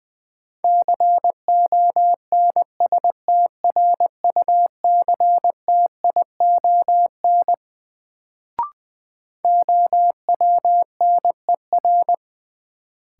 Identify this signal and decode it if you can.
Morse code, heard as a keyed tone, about 20 wpm, 700 Hz. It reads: CONSTRUCTION OWNER
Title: Sentences from Top 1600 Words - New Word Drill Alt - 20wpm